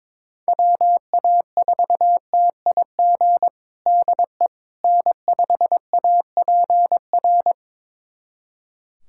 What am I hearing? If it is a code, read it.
Morse code, heard as a keyed tone, about 21 wpm, 700 Hz. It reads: WA4TIG DE N5APR
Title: USA Callsign DE Callsign 22wpm